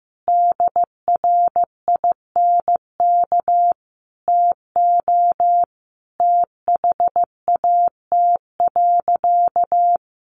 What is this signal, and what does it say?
Morse code, heard as a keyed tone, about 15 wpm, 700 Hz. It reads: DRINK TO THAT.
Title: Sentences from Top 1500 Words 15wpm